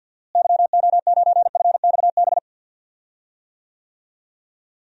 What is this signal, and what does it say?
Morse code, heard as a keyed tone, about 46 wpm, 700 Hz. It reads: YO9FXB
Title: DXCC-100 Call Signs - 6 Characters - Repeated 3x - 50wpm